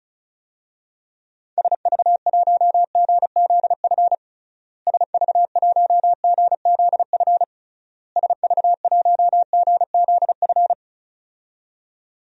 Morse code, heard as a keyed tone, about 34 wpm, 700 Hz. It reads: SV1GZF SV1GZF SV1GZF
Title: DXCC-100 Call Signs - 6 Characters - Repeated 3x - 35wpm